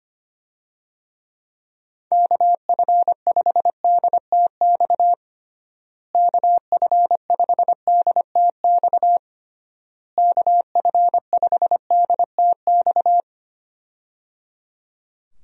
Morse code, heard as a keyed tone, about 25 wpm, 700 Hz. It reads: KF5DTX KF5DTX KF5DTX
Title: USA Call Signs - 6 Characters - Repeated 3x - 25wpm